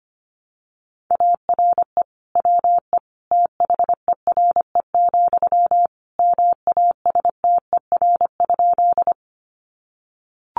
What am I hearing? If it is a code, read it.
Morse code, heard as a keyed tone, about 25 wpm, 700 Hz. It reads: ARE WE THERE, MASTER?
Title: Sentences from Top 1300 Words 25wpm